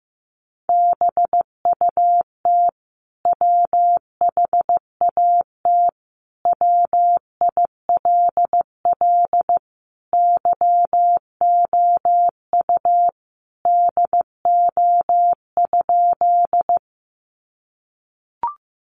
Morse code, heard as a keyed tone, about 15 wpm, 700 Hz. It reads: BUT WHAT WILL YOU DO?
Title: Sentences from Top 100 Words 15wpm